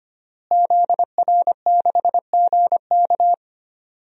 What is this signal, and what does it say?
Morse code, heard as a keyed tone, about 25 wpm, 700 Hz. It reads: ZR6GK
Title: DXCC-100 Call Signs - 5 Characters - Repeated 3x - 25wpm